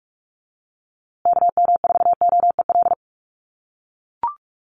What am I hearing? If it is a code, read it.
Morse code, heard as a keyed tone, about 44 wpm, 700 Hz. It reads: KM4OEL
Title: USA Call Signs Repeated 2x 45wpm